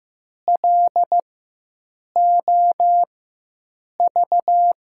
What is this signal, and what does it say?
Morse code, heard as a keyed tone, about 15 wpm, spaced as for 7 wpm, 700 Hz. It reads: LOV
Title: USA Call Signs - 4x Character Spacing - 15wpm